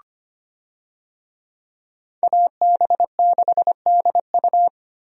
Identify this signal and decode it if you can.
Morse code, heard as a keyed tone, about 25 wpm, 700 Hz. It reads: AB6DU
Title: USA Call Signs - 5 Characters - Repeated 2x - 25wpm